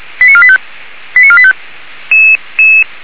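Five-tone selective call (ZVEI), heard as quick runs of five tones, tones of about 70 ms. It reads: 89485 89485
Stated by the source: Received by SDR, ZVEI five-tone selective call 89485.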